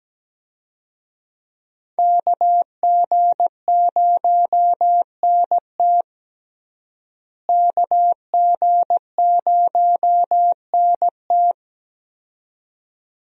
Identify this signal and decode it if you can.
Morse code, heard as a keyed tone, about 17 wpm, 700 Hz. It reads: KG0NT KG0NT
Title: USA Call Signs - 5 Characters - Repeated 2x - 17wpm